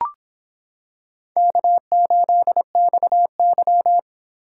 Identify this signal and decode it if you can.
Morse code, heard as a keyed tone, about 25 wpm, 700 Hz. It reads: K8XY